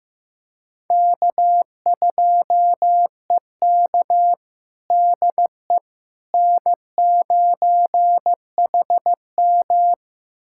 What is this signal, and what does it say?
Morse code, heard as a keyed tone, about 15 wpm, 700 Hz. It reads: K2EK DE N9HM
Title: USA Callsign DE Callsign - 4 Characters - 15wpm